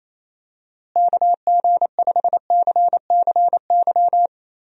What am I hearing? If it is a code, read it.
Morse code, heard as a keyed tone, about 28 wpm, 700 Hz. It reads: KG5CCY
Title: USA Call Signs - 6 Characters - Repeated 2x - 28wpm